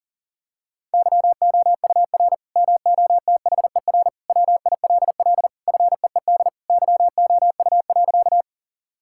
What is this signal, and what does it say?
Morse code, heard as a keyed tone, about 40 wpm, 700 Hz. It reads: YOUR MOTHER WILL FEED YOU.
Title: Sentences from Top 1600 Words 40wpm